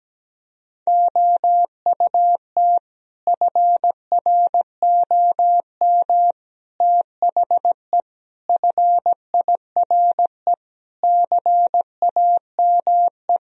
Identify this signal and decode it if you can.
Morse code, heard as a keyed tone, about 17 wpm, 700 Hz. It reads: OUT FROM THE FIRE CAME
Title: Sentences from Top 1000 Words - Review Part 3 - 17wpm